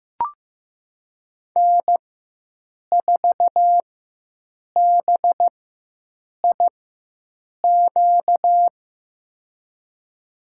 Morse code, heard as a keyed tone, about 15 wpm, spaced as for 7 wpm, 700 Hz. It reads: N4BIQ
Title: USA Call Signs - 5 Characters - 4x Character Spacing - 15wpm